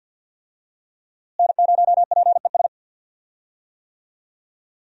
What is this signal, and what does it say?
Morse code, heard as a keyed tone, about 52 wpm, 700 Hz. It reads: N0PES
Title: USA Call Signs - 5 Characters - Repeated 3x - 50wpm